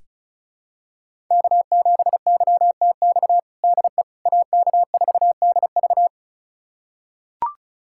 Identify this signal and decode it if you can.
Morse code, heard as a keyed tone, about 36 wpm, 700 Hz. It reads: K7YTX DE AK4DV
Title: USA Callsign DE Callsign - 5 Characters - 35wpm